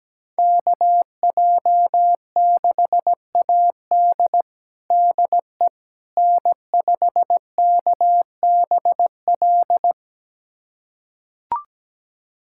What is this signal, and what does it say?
Morse code, heard as a keyed tone, about 17 wpm, 700 Hz. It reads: KJ6AD DE N5KBL
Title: USA Callsign DE Callsign - 5 Characters - 17wpm